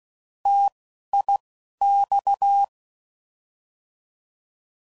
Morse code, heard as a keyed tone, about 16 wpm, 800 Hz. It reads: T I X
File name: Mixed / Contrasts / 3LtrContrasts